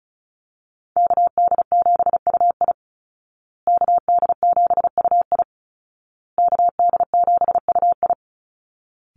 Morse code, heard as a keyed tone, about 34 wpm, 700 Hz. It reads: KD7UI KD7UI KD7UI
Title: Call Signs Repeated 2x - 35wpm